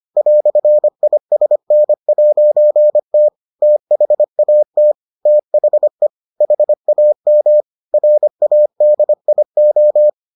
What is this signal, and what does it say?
Morse code, heard as a keyed tone, about 25 wpm, 600 Hz. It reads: "ISN'T THAT THE HAM RADIO